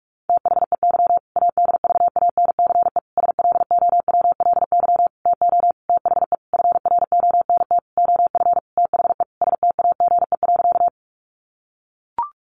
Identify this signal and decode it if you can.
Morse code, heard as a keyed tone, about 44 wpm, 700 Hz. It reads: THEY ADVANCE SLOWLY TO THE FRONT OF THE STAGE.
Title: Sentences from Top 1400 Words 45wpm